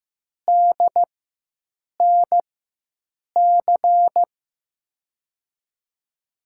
Morse code, heard as a keyed tone, about 15 wpm, spaced as for 7 wpm, 700 Hz. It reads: DNC
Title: USA Call Signs - 6 Characters - 4x Character Spacing - 15wpm